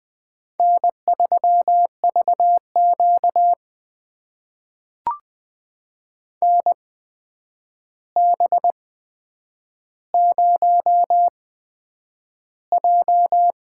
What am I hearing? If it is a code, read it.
Morse code, heard as a keyed tone, about 20 wpm, spaced as for 9 wpm, 700 Hz. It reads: N3VQ NB0J
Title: USA Call Signs - 4 Characters - 8x Character Spacing - 20wpm